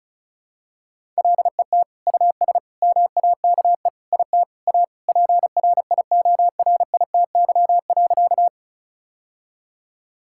Morse code, heard as a keyed tone, about 36 wpm, 700 Hz. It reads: LET US MAKE IT A PRIORITY.
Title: Sentences from Top 1600 Words 35wpm